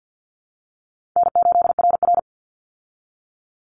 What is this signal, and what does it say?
Morse code, heard as a keyed tone, about 50 wpm, 700 Hz. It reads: N8RR
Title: Call Signs - 50wpm